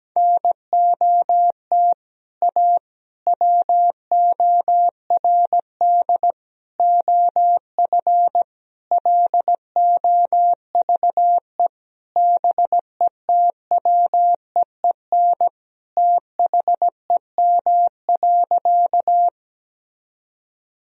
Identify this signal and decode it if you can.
Morse code, heard as a keyed tone, about 17 wpm, 700 Hz. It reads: NOT A WORD OF LOVE BETWEEN THEM.
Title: Sentences from Top 1000 Words - Review Part 1 - 17wpm